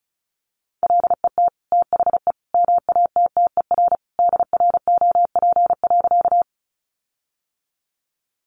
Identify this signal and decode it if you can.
Morse code, heard as a keyed tone, about 34 wpm, 700 Hz. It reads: LET THE MATTER DROP.
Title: Sentences from Top 1400 Words 35wpm